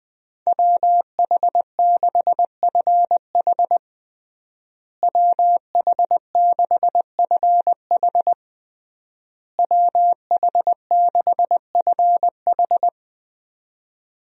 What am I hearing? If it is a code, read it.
Morse code, heard as a keyed tone, about 20 wpm, 700 Hz. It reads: WH6FH WH6FH WH6FH
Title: USA Call Signs - 5 Characters - Repeated 3x - 20wpm